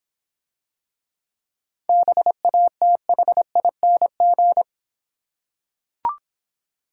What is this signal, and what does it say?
Morse code, heard as a keyed tone, about 26 wpm, 700 Hz. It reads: BATHING